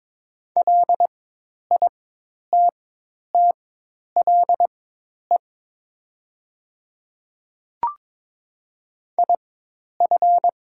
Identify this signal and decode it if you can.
Morse code, heard as a keyed tone, about 22 wpm, spaced as for 10 wpm, 700 Hz. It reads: LITTLE IF
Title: 4x Character Spacing - Top 200 Words - 22wpm